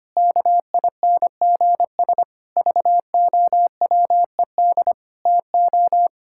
Morse code, heard as a keyed tone, about 25 wpm, 700 Hz. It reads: KINGS VOWED TO
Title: Pangram Challenge v2 - 25wpm